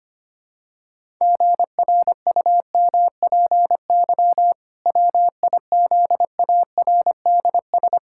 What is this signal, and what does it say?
Morse code, heard as a keyed tone, about 25 wpm, 700 Hz. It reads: GRUMPY WIZARDS
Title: Pangram Challenge v2 - 25wpm